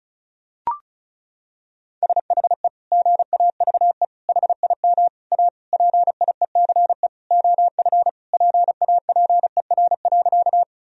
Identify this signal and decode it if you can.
Morse code, heard as a keyed tone, about 34 wpm, 700 Hz. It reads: SHE GAVE HIM A PIECE OF PAPER.
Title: Sentences from Top 1000 Words - Review Part 2 - 35wpm